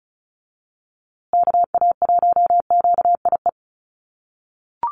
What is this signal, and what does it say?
Morse code, heard as a keyed tone, about 34 wpm, 700 Hz. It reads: KA1QIE
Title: USA Call Signs Repeated 3x 35wpm